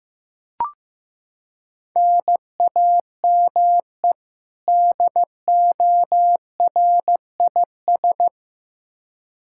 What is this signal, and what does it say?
Morse code, heard as a keyed tone, about 15 wpm, 700 Hz. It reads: NAME DORIS
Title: Practice QSO - Exchange 1 - Beginning - 15wpm